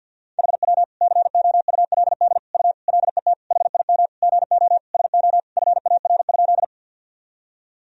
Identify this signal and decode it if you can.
Morse code, heard as a keyed tone, about 52 wpm, 700 Hz. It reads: HW COULD U LET HIM GO SO FAR?
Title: Sentences from Top 300 Words and Abbreviations 50wpm